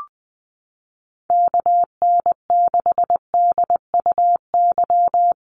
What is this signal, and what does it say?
Morse code, heard as a keyed tone, about 20 wpm, 700 Hz. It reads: KN6DUY
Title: USA Call Signs Repeated 3x 20wpm